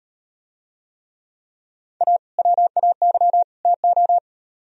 Morse code, heard as a keyed tone, about 38 wpm, 700 Hz. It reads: A WAY TO